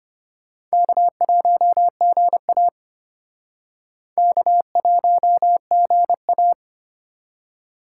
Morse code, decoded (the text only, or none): K1GA K1GA